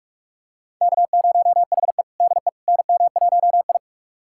Morse code, heard as a keyed tone, about 46 wpm, 690 Hz. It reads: K0HE DE NM1I
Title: USA Callsign DE Callsign - 4 Characters - 45wpm